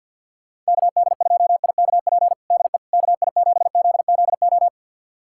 Morse code, heard as a keyed tone, about 53 wpm, 700 Hz. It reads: KD2IKP DE KI7ZZO
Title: USA Callsign DE Callsign - 6 Characters - 50wpm